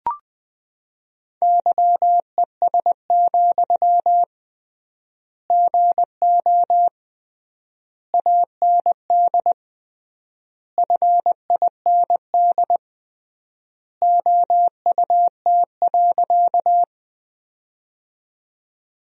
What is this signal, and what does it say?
Morse code, heard as a keyed tone, about 20 wpm, 700 Hz. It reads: YES, GO AND FIND OUT.
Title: 3x Word-Spacing - Sentences from Top 500 Words - 20wpm